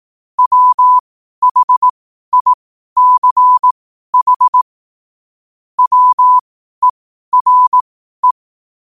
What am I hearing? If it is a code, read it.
Morse code, heard as a keyed tone, about 18 wpm, spaced as for 12 wpm, 1000 Hz. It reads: WHICH WERE
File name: qotdfiles / 12wpm